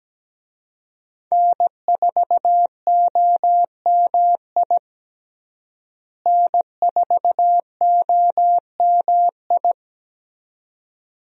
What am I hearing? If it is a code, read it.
Morse code, heard as a keyed tone, about 17 wpm, 700 Hz. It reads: N4OMI N4OMI
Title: USA Call Signs - 5 Characters - Repeated 2x - 17wpm